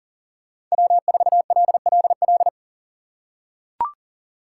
Morse code, heard as a keyed tone, about 40 wpm, 700 Hz. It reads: W4LLL
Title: USA Call Signs - 5 Characters - 2x Character Spacing - 40wpm